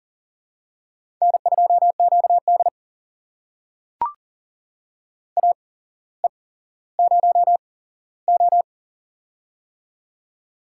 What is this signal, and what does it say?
Morse code, heard as a keyed tone, about 39 wpm, spaced as for 17 wpm, 700 Hz. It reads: N2QD AE0O
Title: USA Call Signs - 4 Characters - 8x Character Spacing - 40wpm